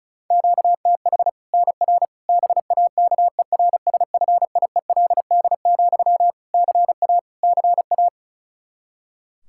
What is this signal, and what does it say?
Morse code, heard as a keyed tone, about 36 wpm, 700 Hz. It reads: QTH NR BAKERSFIELD, CA CA
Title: Practice QSO - Exchange 1 - Beginning - 35wpm